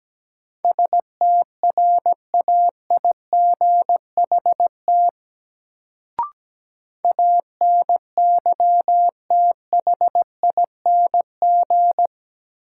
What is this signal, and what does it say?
Morse code, heard as a keyed tone, about 17 wpm, 700 Hz. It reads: STRAIGHT ANYTHING
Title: Sentences from Top 1000 Words - New Word Drill - 17wpm